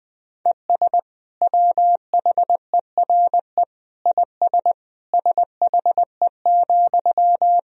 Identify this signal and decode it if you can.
Morse code, heard as a keyed tone, about 21 wpm, 700 Hz. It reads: ES WHERE IS SHE,
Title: Sentences from Top 200 Words and Abbreviations 20wpm